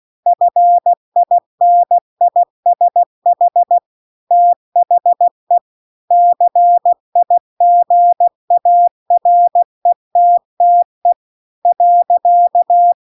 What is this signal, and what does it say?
Morse code, heard as a keyed tone, about 16 wpm, 700 Hz. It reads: FINISH THE CIGARETTE .